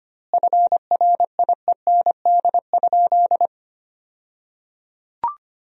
Morse code, heard as a keyed tone, about 25 wpm, 700 Hz. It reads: FRIEND?
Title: 2x Word-Spacing - Sentences from Top 300 Words - 25wpm